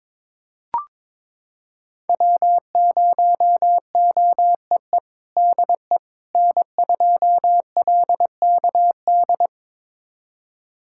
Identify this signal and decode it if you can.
Morse code, heard as a keyed tone, about 22 wpm, 700 Hz. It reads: W0OEE DE N2LKD
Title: USA Callsign DE Callsign - 5 Characters - 22wpm